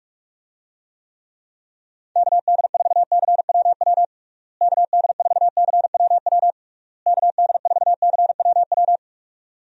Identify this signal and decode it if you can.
Morse code, heard as a keyed tone, about 46 wpm, 700 Hz. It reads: KD4CWW KD4CWW KD4CWW
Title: USA Call Signs - 6 Characters - Repeated 3x - 45wpm